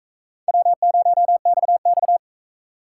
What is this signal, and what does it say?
Morse code, heard as a keyed tone, about 43 wpm, 700 Hz. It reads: W0XX